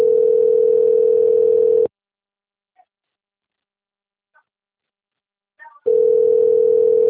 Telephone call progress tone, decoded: ringback tone

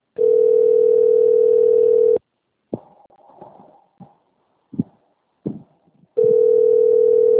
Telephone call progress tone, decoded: ringback tone

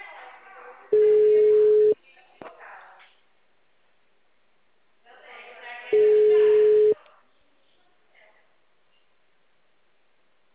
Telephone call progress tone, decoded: ringback tone

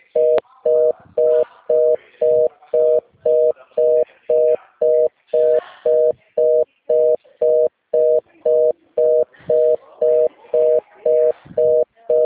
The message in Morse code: none

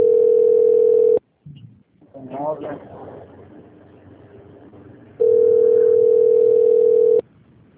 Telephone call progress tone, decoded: ringback tone

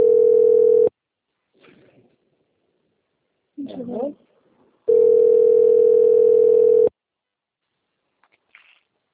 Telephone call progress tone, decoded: ringback tone

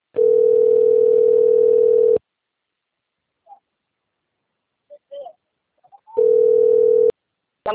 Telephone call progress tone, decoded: ringback tone

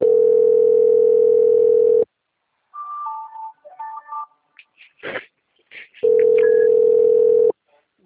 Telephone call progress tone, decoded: ringback tone

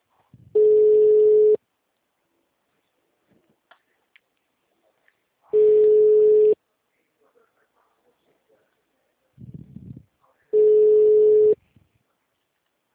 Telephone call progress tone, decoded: ringback tone

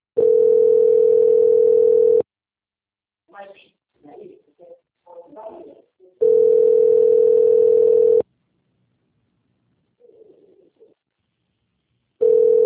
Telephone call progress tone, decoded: ringback tone

